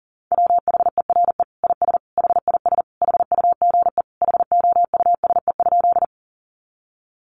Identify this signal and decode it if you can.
Morse code, heard as a keyed tone, about 40 wpm, 700 Hz. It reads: WHERE IS HIS HUGE HOUSE?
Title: Sentences from Top 1100 Words 40wpm